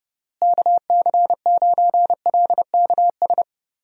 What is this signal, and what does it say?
Morse code, heard as a keyed tone, about 30 wpm, 700 Hz. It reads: KC9LKS
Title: USA Call Signs - 6 Characters - Repeated 3x - 30wpm